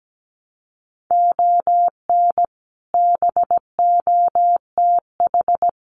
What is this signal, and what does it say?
Morse code, heard as a keyed tone, about 17 wpm, 700 Hz. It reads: ON BOTH
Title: Sets of 4 Words 17wpm